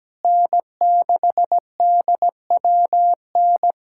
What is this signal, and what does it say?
Morse code, heard as a keyed tone, about 17 wpm, 700 Hz. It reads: N6DWN
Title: USA Callsign DE Callsign - 5 Characters - 17wpm